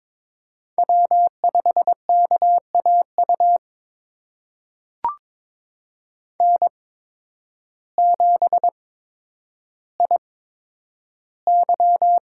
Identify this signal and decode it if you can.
Morse code, heard as a keyed tone, about 22 wpm, spaced as for 10 wpm, 700 Hz. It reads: W5KAU N7IY
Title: USA Call Signs - 8x Character Spacing - 22wpm